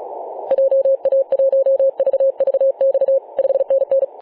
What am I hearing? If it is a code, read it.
Morse code, heard as a keyed tone, about 36 wpm, 540 Hz. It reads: JA1VVX 5NN
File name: 2013/Sep/22 0824z XZ1Z 24.896MHz CW